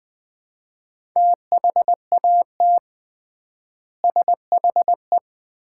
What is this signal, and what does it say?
Morse code, heard as a keyed tone, about 21 wpm, 700 Hz. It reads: THAT SHE
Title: 3x Word-Spacing - Sets of 3 Words - 20wpm